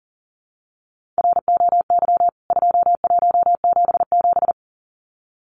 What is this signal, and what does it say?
Morse code, heard as a keyed tone, about 40 wpm, 700 Hz. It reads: ROY 2177
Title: CWT Exchange 40wpm